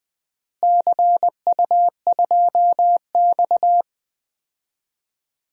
Morse code, heard as a keyed tone, about 20 wpm, 700 Hz. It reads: CU2X
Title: DXCC-100 Call Signs - 4 Characters - Repeated 3x - 20wpm